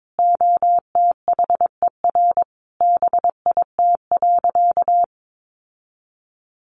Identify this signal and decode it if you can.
Morse code, heard as a keyed tone, about 22 wpm, 700 Hz. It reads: OTHER BIT.
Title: Sentences from Top 900 Words 22wpm